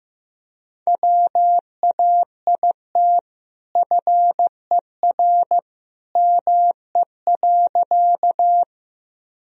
Morse code, heard as a keyed tone, about 15 wpm, 700 Hz. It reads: WAIT FER ME.
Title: Sentences from Top 500 Words and Abbreviations 15wpm